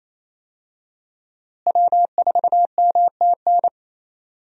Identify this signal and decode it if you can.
Morse code, heard as a keyed tone, about 28 wpm, 700 Hz. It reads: W4MTN
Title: USA Call Signs - 5 Characters - Repeated 2x - 28wpm